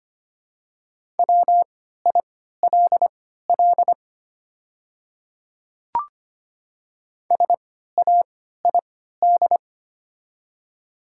Morse code, heard as a keyed tone, about 25 wpm, spaced as for 14 wpm, 700 Hz. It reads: WILL SAID
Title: 3x Character Spacing - Top 200 Words - 25wpm